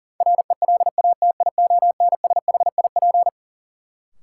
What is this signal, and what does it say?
Morse code, heard as a keyed tone, about 40 wpm, 700 Hz. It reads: RELATIONSHIP
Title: Sentences from Top 1100 Words - New Word Drill Alt - 40wpm